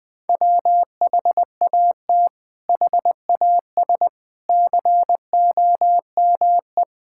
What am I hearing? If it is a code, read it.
Morse code, heard as a keyed tone, about 20 wpm, 700 Hz. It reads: WHAT HAS COME
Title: Sets of 8 Words - 20wpm